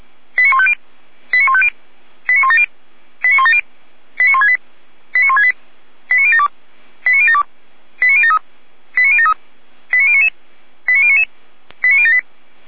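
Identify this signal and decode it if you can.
Five-tone selective call (ZVEI), heard as quick runs of five tones, tones of about 70 ms. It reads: 89150 89150 89170 89170 89168 89168 89072 89072 89073 89073 89099 89099 89078
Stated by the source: ZVEI five-tone call 89169 received by SDR.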